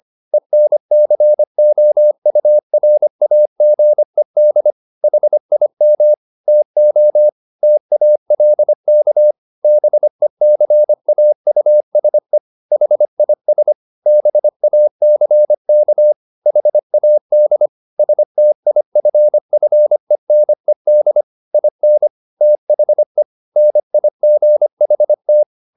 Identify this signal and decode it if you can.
Morse code, heard as a keyed tone, about 25 wpm, 600 Hz. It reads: ENCOURAGED HIM TO TALK BECAUSE HIS BACK HAD STIFFENED IN THE NIGHT
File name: the old man and the sea part 1 25wpm